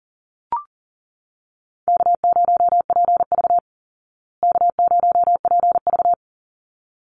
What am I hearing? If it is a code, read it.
Morse code, heard as a keyed tone, about 40 wpm, 700 Hz. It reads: K0PV K0PV
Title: Easy Call Signs Repeated 2x - 40wpm